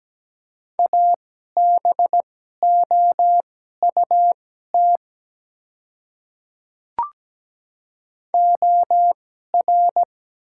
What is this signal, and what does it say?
Morse code, heard as a keyed tone, about 17 wpm, spaced as for 12 wpm, 700 Hz. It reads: ABOUT OR
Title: 2x Character Spacing - Top 300 Words - 17wpm